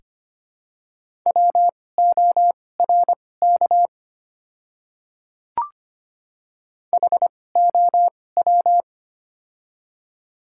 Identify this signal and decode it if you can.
Morse code, heard as a keyed tone, about 25 wpm, spaced as for 18 wpm, 700 Hz. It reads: WORK HOW
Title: 2x Character Spacing - Top 500 Words - 25wpm